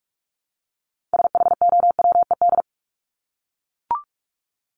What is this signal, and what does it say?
Morse code, heard as a keyed tone, about 44 wpm, 700 Hz. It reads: SHOWED
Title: ED Suffix - Top 100 Words with Base Words 45wpm